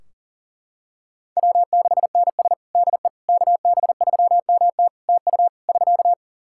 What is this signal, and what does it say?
Morse code, heard as a keyed tone, about 40 wpm, 700 Hz. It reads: W6NS DE KB3MT TU <SK>
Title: Practice QSO - Exchange 4 - Ending - 40wpm